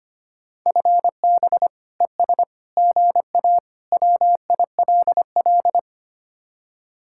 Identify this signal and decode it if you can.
Morse code, heard as a keyed tone, about 33 wpm, 700 Hz. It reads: FB ES GA WILL